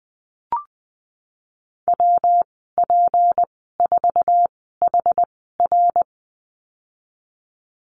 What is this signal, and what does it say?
Morse code, heard as a keyed tone, about 20 wpm, spaced as for 14 wpm, 700 Hz. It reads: WP4HR
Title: 167 Lesson 27g - Number 7 - Call Signs 20wpm